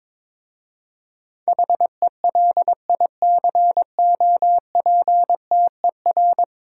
Morse code, heard as a keyed tone, about 22 wpm, 700 Hz. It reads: HELICOPTER
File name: practice / 22wpm / 10 Letter Words - First 3 Letters Given - 22wpm